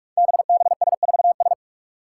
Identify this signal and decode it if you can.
Morse code, heard as a keyed tone, about 44 wpm, 700 Hz. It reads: BBS4S
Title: Five Challenging Letter-Number Combinations 45wpm